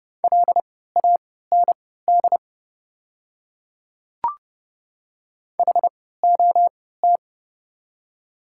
Morse code, heard as a keyed tone, about 30 wpm, spaced as for 17 wpm, 690 Hz. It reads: LAND HOT
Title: 3x Character Spacing - Top 200 Words - 30wpm